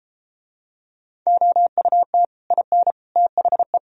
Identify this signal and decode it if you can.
Morse code, heard as a keyed tone, about 31 wpm, 700 Hz. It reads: OUT IN THE